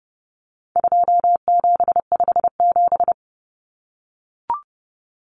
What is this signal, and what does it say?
Morse code, heard as a keyed tone, about 30 wpm, 700 Hz. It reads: 2757
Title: Common Cut Numbers - 4-Digit 30wpm